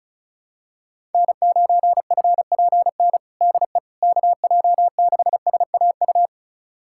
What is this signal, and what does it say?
Morse code, heard as a keyed tone, about 36 wpm, 700 Hz. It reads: N9FPN DE KJ6SAU
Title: USA Callsign DE Callsign 35wpm